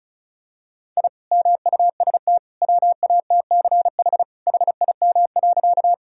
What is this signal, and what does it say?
Morse code, heard as a keyed tone, about 36 wpm, 700 Hz. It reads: I MUST WATCH HIM.
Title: Sentences from Top 1000 Words - Review Part 1 - 35wpm